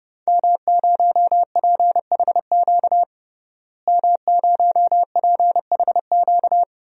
Morse code, heard as a keyed tone, about 32 wpm, 700 Hz. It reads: M0PHQ M0PHQ
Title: DXCC-100 Call Signs - 5 Characters - Repeated 3x - 30wpm